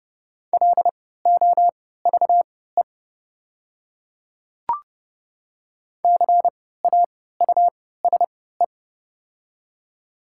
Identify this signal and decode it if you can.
Morse code, heard as a keyed tone, about 30 wpm, spaced as for 17 wpm, 690 Hz. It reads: LOVE CAUSE
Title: 3x Character Spacing - Top 500 Words - 30wpm